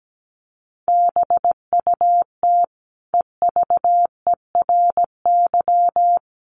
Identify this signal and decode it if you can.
Morse code, heard as a keyed tone, about 17 wpm, 700 Hz. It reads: BUT EVERY
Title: Sentences from Top 1100 Words 17wpm